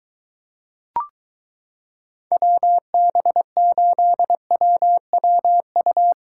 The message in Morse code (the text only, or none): WB8WWU